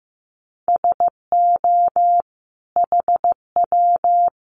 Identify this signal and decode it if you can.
Morse code, heard as a keyed tone, about 15 wpm, 700 Hz. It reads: SO HW
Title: Common QSO Phrases 15wpm